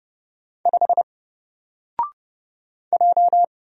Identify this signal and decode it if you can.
Morse code, heard as a keyed tone, about 30 wpm, 700 Hz. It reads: 5 J
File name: Single-Letter-Number-30wpm